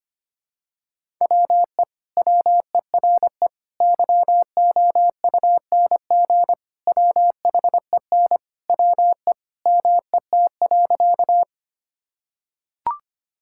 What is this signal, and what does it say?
Morse code, heard as a keyed tone, about 25 wpm, 700 Hz. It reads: WE WERE YOUNG WHEN WE MET.
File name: practice / 25wpm / Sentences with 2-Word Phrases - 25wpm